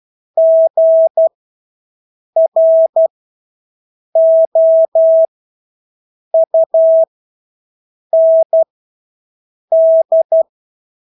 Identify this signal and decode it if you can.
Morse code, heard as a keyed tone, about 12 wpm, spaced as for 6 wpm, 650 Hz. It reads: GROUND
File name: Engelske ord 12-6 wpm